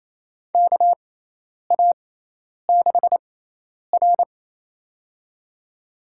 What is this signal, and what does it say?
Morse code, heard as a keyed tone, about 28 wpm, spaced as for 10 wpm, 700 Hz. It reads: KA6R
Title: USA Call Signs - 6x Character Spacing - 28wpm